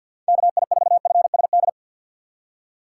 Morse code, heard as a keyed tone, about 52 wpm, 700 Hz. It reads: KI4FSD
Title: USA Call Signs - 6 Characters - 4x Character Spacing - 50wpm